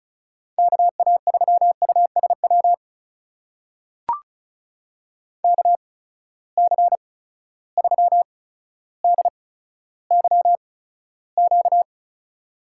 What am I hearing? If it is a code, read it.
Morse code, heard as a keyed tone, about 36 wpm, spaced as for 15 wpm, 700 Hz. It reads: KA3USW KC3DYQ
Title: USA Call Signs - 6 Characters - 8x Character Spacing - 35wpm